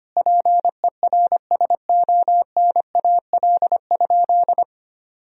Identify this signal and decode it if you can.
Morse code, heard as a keyed tone, about 25 wpm, 700 Hz. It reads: PERSONAL?
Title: Sentences from Top 1000 Words - Review Part 4 - 25wpm